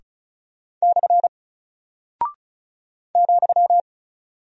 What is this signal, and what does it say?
Morse code, heard as a keyed tone, about 34 wpm, 700 Hz. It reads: / ,
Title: Single Character (Letter:Number:Punctuation:Prosign) Alt 35wpm